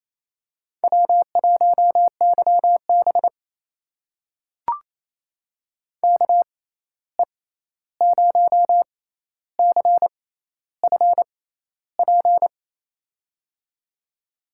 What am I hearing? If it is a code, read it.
Morse code, heard as a keyed tone, about 28 wpm, spaced as for 13 wpm, 700 Hz. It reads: W1YB KE0CFP
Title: USA Call Signs - 6x Character Spacing - 28wpm